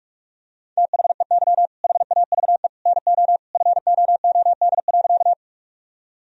Morse code, heard as a keyed tone, about 44 wpm, 700 Hz. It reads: THEY HAVE NO FOOD.
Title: Sentences from Top 1000 Words - Review Part 1 - 45wpm